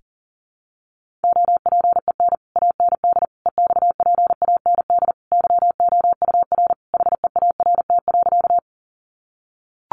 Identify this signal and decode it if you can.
Morse code, heard as a keyed tone, about 40 wpm, 700 Hz. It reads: OPEN AND EXPAND YOUR HEART.
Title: Sentences from Top 1400 Words 40wpm